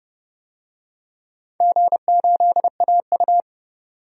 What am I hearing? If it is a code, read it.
Morse code, heard as a keyed tone, about 32 wpm, 700 Hz. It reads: G8AU
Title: DXCC-100 Call Signs - 4 Characters - 30wpm